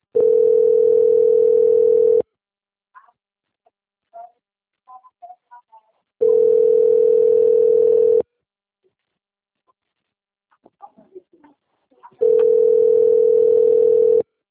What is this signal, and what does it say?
Ringback tone (North America), 440 + 480 Hz.
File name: out- - - - - .24762